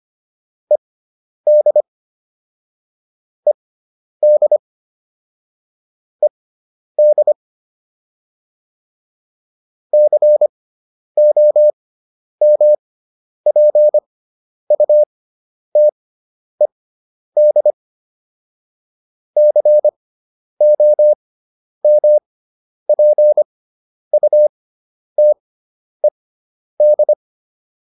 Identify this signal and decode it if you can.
Morse code, heard as a keyed tone, about 25 wpm, spaced as for 10 wpm, 600 Hz. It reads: ED ED ED COMPUTED COMPUTED